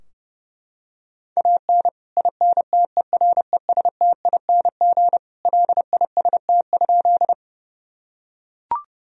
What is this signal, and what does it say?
Morse code, heard as a keyed tone, about 30 wpm, 700 Hz. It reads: AN INTERESTING LIST?
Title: Sentences from Top 1000 Words - Review Part 2 - 30wpm